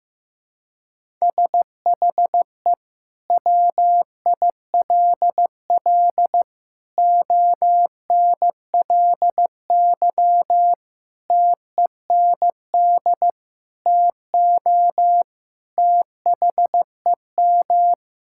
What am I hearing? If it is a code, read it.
Morse code, heard as a keyed tone, about 15 wpm, 700 Hz. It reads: SHE WILL ONLY TEND TO THEM
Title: Sentences from Top 1000 Words - Review Part 5 - 15wpm